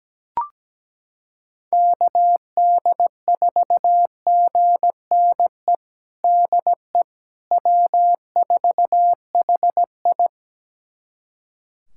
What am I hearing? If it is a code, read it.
Morse code, heard as a keyed tone, about 17 wpm, 700 Hz. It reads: KD4GNE DE W4HI
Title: USA Callsign DE Callsign 17wpm